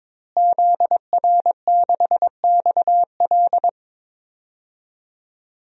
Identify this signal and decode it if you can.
Morse code, heard as a keyed tone, about 22 wpm, 700 Hz. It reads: ZR6XL
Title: DXCC-100 Call Signs - 5 Characters - Repeated 2x - 22wpm